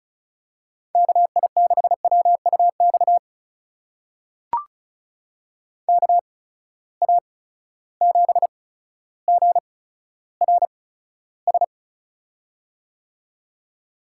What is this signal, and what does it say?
Morse code, heard as a keyed tone, about 36 wpm, spaced as for 15 wpm, 700 Hz. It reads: KI6WUX KA7GRS
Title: USA Call Signs - 6 Characters - 8x Character Spacing - 35wpm